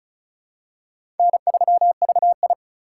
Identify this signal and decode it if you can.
Morse code, heard as a keyed tone, about 34 wpm, 700 Hz. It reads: N3VI
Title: USA Call Signs - 4 Characters - 2x Character Spacing - 35wpm